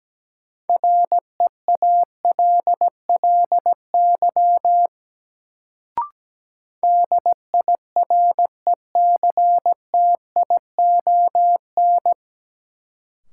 Morse code, heard as a keyed tone, about 17 wpm, 700 Hz. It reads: REALLY DIRECTION
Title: Sets of 4 Words - New Word Drill - 17wpm